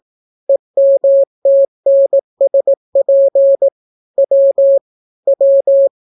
Morse code, heard as a keyed tone, about 18 wpm, 550 Hz. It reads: EMTNSP W W